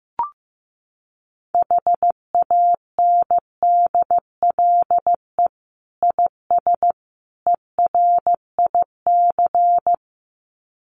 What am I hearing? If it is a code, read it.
Morse code, heard as a keyed tone, about 14 wpm, 700 Hz. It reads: HANDLE IS ERIC
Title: Common QSO Phrases 15wpm